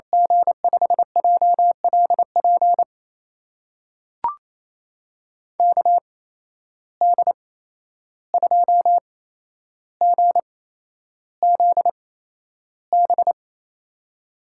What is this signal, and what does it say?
Morse code, heard as a keyed tone, about 28 wpm, spaced as for 11 wpm, 700 Hz. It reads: G5JLP KD2GZB
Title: USA Call Signs - 6 Characters - 8x Character Spacing - 28wpm